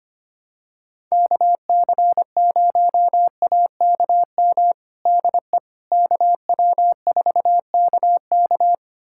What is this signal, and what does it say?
Morse code, heard as a keyed tone, about 25 wpm, 700 Hz. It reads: KC0AKM DE KW4KK
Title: USA Callsign DE Callsign 25wpm